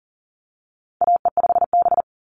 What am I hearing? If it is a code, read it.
Morse code, heard as a keyed tone, about 38 wpm, 700 Hz. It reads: AE5B
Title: NA SOTA Chaser Callsigns 40wpm